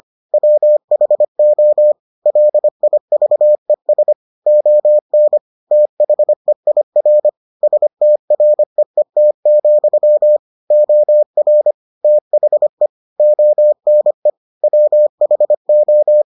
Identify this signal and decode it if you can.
Morse code, heard as a keyed tone, about 25 wpm, 600 Hz. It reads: WHO LIVES ON THEIR STREET, OR THE ONE WHO